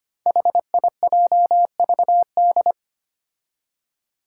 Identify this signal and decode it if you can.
Morse code, heard as a keyed tone, about 25 wpm, 700 Hz. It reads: HIJVD
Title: Five Challenging Letter-Number Combinations 25wpm